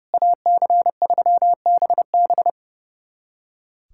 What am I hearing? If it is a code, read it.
Morse code, heard as a keyed tone, about 30 wpm, 700 Hz. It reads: AC3BB
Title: USA Callsign DE Callsign - 5 Characters - 30wpm